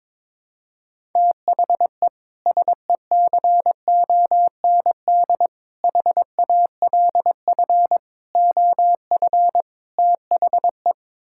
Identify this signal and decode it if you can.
Morse code, heard as a keyed tone, about 22 wpm, 700 Hz. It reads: THE SECOND HALF OF THE
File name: practice / 22wpm / Build A Phrase - 5-Word Phrases - 22wpm